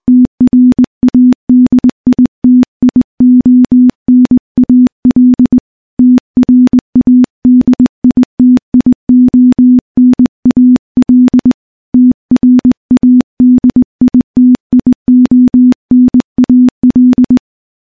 Morse code, heard as a keyed tone, about 20 wpm, 250 Hz. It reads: TRADITIONAL TRADITIONAL TRADITIONAL
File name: morse / morse-training-20WPM